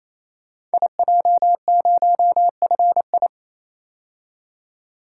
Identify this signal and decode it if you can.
Morse code, heard as a keyed tone, about 27 wpm, 700 Hz. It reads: IJ0FI
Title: Code Groups - 5 Letter:Numbers - 28wpm